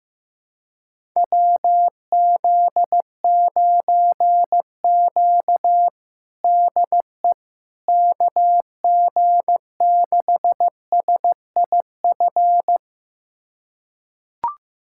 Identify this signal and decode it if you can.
Morse code, heard as a keyed tone, about 15 wpm, 700 Hz. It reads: WZ9Q DE KG6SIF
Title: USA Callsign DE Callsign 15wpm